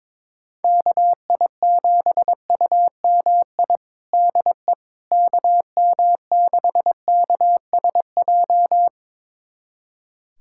Morse code, heard as a keyed tone, about 22 wpm, 700 Hz. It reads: KI7UMI DE KM6KSJ
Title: USA Callsign DE Callsign - 6 Characters - 22wpm